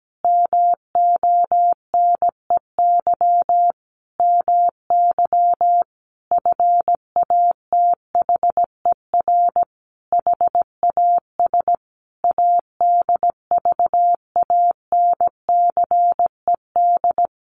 Morse code, heard as a keyed tone, about 17 wpm, 700 Hz. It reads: MONEY MY FATHER HAS ADVANCED
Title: Sentences from Top 1200 Words 17wpm